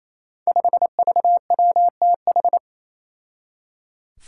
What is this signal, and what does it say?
Morse code, heard as a keyed tone, about 28 wpm, 700 Hz. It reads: 5VWTH
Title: Five Challenging Letter-Number Combinations 28wpm